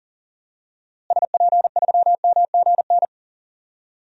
Morse code, heard as a keyed tone, about 40 wpm, 700 Hz. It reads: SP3MGN
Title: DXCC-100 Call Signs - 6 Characters - Repeated 3x - 40wpm